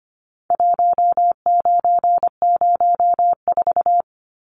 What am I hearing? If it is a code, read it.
Morse code, heard as a keyed tone, about 25 wpm, 700 Hz. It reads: 1904
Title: Numbers - 4-Digit 25wpm